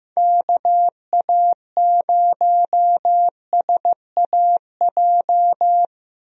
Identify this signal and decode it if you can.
Morse code, heard as a keyed tone, about 15 wpm, 700 Hz. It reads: KA0SAJ
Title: USA Call Signs - 6 Characters - Repeated 3x - 15wpm